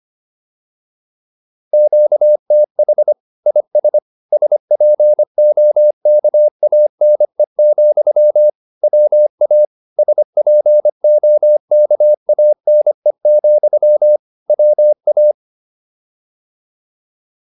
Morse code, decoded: QTH IS SPOKANE, WA SPOKANE, WA

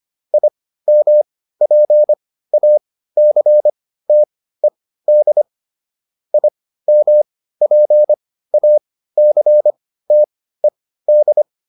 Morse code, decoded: IMPACTED IMPACTED

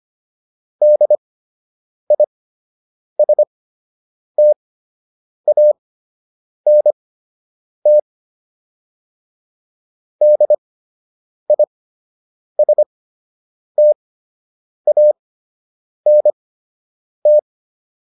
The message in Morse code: DISTANT DISTANT

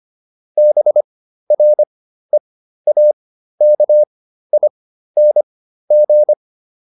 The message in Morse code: BREAKING

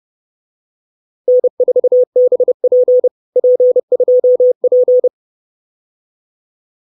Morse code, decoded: N4BP P2P

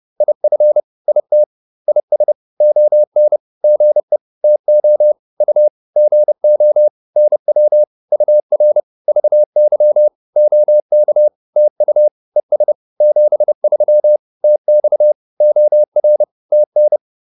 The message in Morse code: IF IT IS ON GE TO U GO NW UR VY OK TU ES 73 TX OR TN

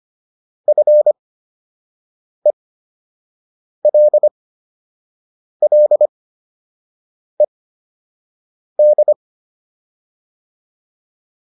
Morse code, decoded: FELLED

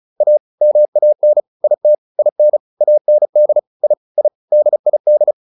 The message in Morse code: A MAN IT IN AND I I DID

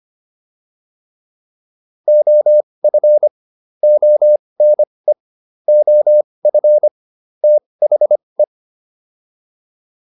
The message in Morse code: OF ONE OF THE

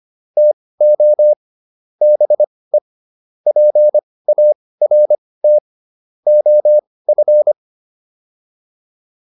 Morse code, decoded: TO BE PART OF

